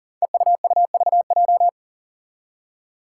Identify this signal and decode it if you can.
Morse code, heard as a keyed tone, about 40 wpm, 700 Hz. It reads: EUUVJ
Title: Five Challenging Letter-Number Combinations 40wpm